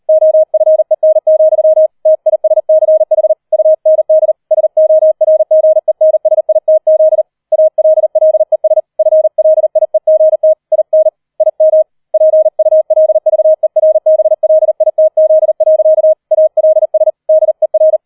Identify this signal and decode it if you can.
Morse code, heard as a keyed tone, about 38 wpm, 630 Hz. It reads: OFEN, TISCH UND SORGENSITZ ALLES FLIEGT IN IM PULVERBLITZ. ALS DER
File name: 004 Vierter Streich 40wpm